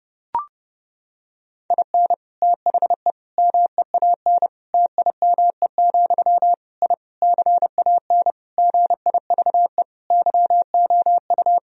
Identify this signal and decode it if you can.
Morse code, heard as a keyed tone, about 30 wpm, 700 Hz. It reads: IN THE MEAN TIME, I CAN GIVE YOU
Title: Sentences from Top 1000 Words - Review Part 3 - 30wpm